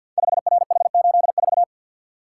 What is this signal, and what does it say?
Morse code, heard as a keyed tone, about 48 wpm, 700 Hz. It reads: 5RH84
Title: Code Groups - 5 Letter:Numbers - 50wpm